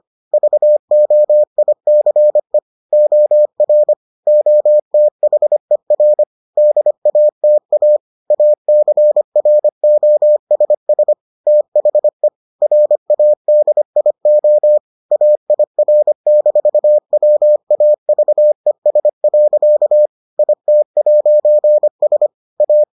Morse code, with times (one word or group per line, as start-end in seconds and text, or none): 0.3-2.6 VOICE
2.9-3.9 OR
4.3-6.2 OTHER
6.6-8.0 DATA
8.3-11.1 ACROSS
11.5-12.3 THE
12.6-14.8 RADIO
15.1-20.1 AIR-WAVES.
20.4-22.3 IT'S
22.6-22.8 A